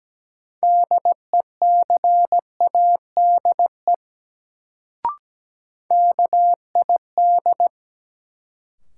Morse code, heard as a keyed tone, about 17 wpm, 700 Hz. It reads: DECADE KID
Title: Sets of 5 Words - New Word Drill - 17wpm